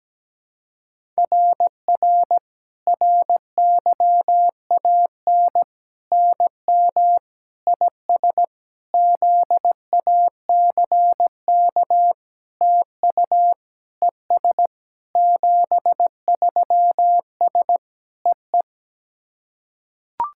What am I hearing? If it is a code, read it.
Morse code, heard as a keyed tone, about 17 wpm, 700 Hz. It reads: RR RYAN NM IS ZACK TU ES 73S EE
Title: SOTA QSO Practice 17wpm